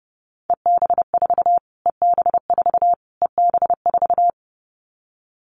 Morse code, heard as a keyed tone, about 30 wpm, 700 Hz. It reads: EB4 EB4 EB4
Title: Callsign Prefixes for Top 25 DXCC - Repeated 3x - 30wpm